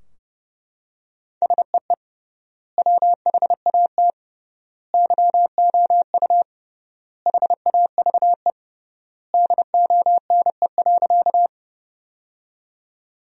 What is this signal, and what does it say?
Morse code, heard as a keyed tone, about 32 wpm, 700 Hz. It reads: SEE WHAT YOU HAVE DONE.
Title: 3x Word-Spacing - Sentences from Top 500 Words - 30wpm